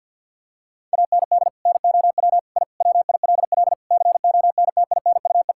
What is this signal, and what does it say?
Morse code, heard as a keyed tone, about 50 wpm, 700 Hz. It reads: AND NOW I WILL CONTINUE
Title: Sentences from Top 1000 Words - Review Part 6 - 50wpm